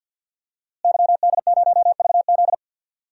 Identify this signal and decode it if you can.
Morse code, heard as a keyed tone, about 47 wpm, 700 Hz. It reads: YD0VZ
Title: DXCC-100 Call Signs - 5 Characters - Repeated 2x - 50wpm